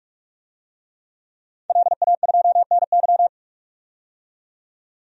Morse code, heard as a keyed tone, about 46 wpm, 700 Hz. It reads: LA2NY
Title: DXCC-100 Call Signs - 5 Characters - 45wpm